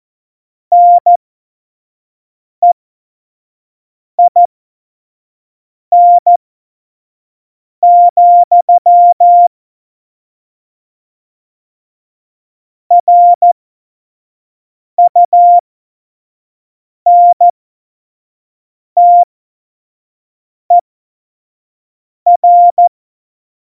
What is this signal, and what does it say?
Morse code, heard as a keyed tone, about 14 wpm, spaced as for 5 wpm, 700 Hz. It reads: NEIN, RUNTER
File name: Morsekurs / Witze / Witz 12. Dezember 2017, 25 F70